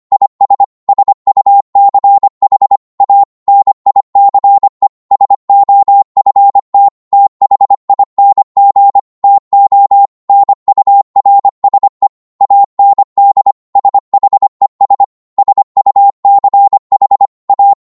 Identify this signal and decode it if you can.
Morse code, heard as a keyed tone, about 25 wpm, 800 Hz. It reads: IS SUCH A NICE SOFT THING TO NURSE AND SHES SUCH A